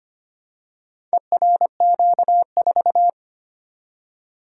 Morse code, heard as a keyed tone, about 25 wpm, 700 Hz. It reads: ERQ4
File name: practice / 25wpm / Four Letter-Number Combinations - 25wpm